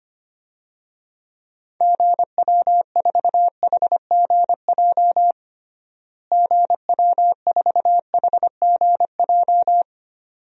Morse code, heard as a keyed tone, about 25 wpm, 700 Hz. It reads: GW4HGJ GW4HGJ
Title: DXCC-100 Call Signs - 6 Characters - Repeated 2x - 25wpm